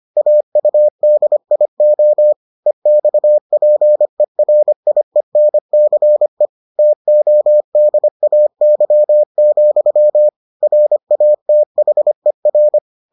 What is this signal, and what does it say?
Morse code, heard as a keyed tone, about 25 wpm, 600 Hz. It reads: AUDIO EXPERIENCE TODAY, RATHER